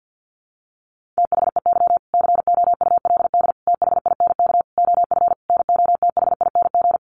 Morse code, heard as a keyed tone, about 50 wpm, 700 Hz. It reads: THEY COULD THINK OF NOTHING
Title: Sentences from Top 800 Words 50wpm